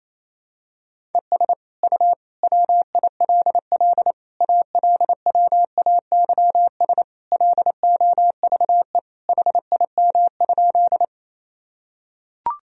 Morse code, heard as a keyed tone, about 28 wpm, 700 Hz. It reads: ES U WILL ALWAYS LOVE HIM?
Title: Sentences from Top 400 Words and Abbreviations 28wpm